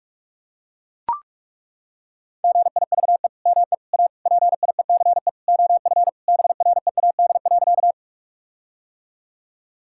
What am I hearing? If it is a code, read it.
Morse code, heard as a keyed tone, about 41 wpm, 700 Hz. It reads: GIVE ME A PIECE OF BREAD.
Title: Sentences from Top 1600 Words 45wpm